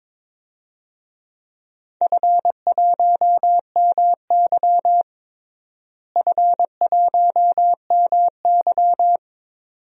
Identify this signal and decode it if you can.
Morse code, heard as a keyed tone, about 22 wpm, 700 Hz. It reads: F1MY F1MY
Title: DXCC-100 Call Signs - 4 Characters - Repeated 2x - 22wpm